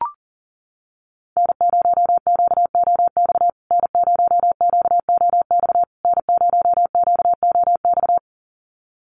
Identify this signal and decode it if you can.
Morse code, heard as a keyed tone, about 37 wpm, 700 Hz. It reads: N0QOX N0QOX N0QOX
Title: USA Call Signs Repeated 3x 40wpm